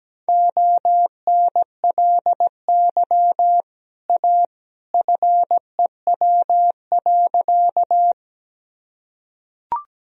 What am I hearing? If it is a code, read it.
Morse code, heard as a keyed tone, about 17 wpm, 700 Hz. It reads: ONLY A FEW.
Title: Sentences from Top 1000 Words - Review Part 1 - 17wpm